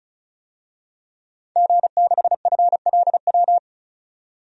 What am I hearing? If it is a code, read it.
Morse code, heard as a keyed tone, about 34 wpm, 700 Hz. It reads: G6FLW
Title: DXCC-100 Call Signs - 5 Characters - 35wpm